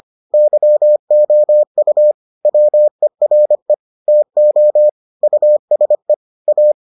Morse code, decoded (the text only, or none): YOU WERE TO USE A